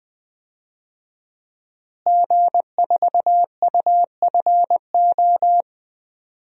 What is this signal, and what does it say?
Morse code, heard as a keyed tone, about 20 wpm, 700 Hz. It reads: G4UFO
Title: DXCC-100 Call Signs - 5 Characters - Repeated 2x - 20wpm